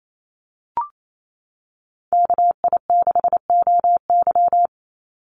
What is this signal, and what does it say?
Morse code, heard as a keyed tone, about 28 wpm, 700 Hz. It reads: KI6OY
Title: Call Signs - 28wpm